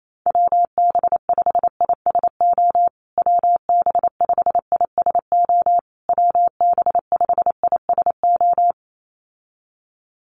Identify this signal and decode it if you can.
Morse code, heard as a keyed tone, about 28 wpm, 700 Hz. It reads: WB5ISO WB5ISO WB5ISO
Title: USA Call Signs Repeated 3x 28wpm